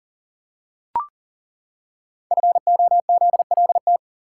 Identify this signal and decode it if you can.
Morse code, heard as a keyed tone, about 42 wpm, 690 Hz. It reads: FOZLT
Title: Code Groups - 5 Letters - 40wpm